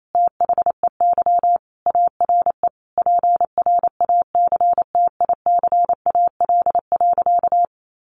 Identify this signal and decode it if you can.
Morse code, heard as a keyed tone, about 28 wpm, 700 Hz. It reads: THEY ARE PRACTICAL.
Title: Sentences from Top 1400 Words 28wpm